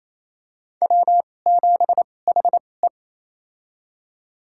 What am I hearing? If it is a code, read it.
Morse code, heard as a keyed tone, about 28 wpm, spaced as for 20 wpm, 700 Hz. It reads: W7HE
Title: USA Call Signs - 4 Characters - 2x Character Spacing - 28wpm